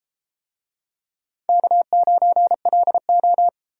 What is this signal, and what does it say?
Morse code, heard as a keyed tone, about 30 wpm, 700 Hz. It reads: K9LO